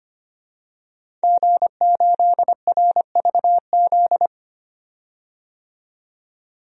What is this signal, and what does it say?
Morse code, heard as a keyed tone, about 25 wpm, 700 Hz. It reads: G8RVZ
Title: DXCC-100 Call Signs - 5 Characters - 25wpm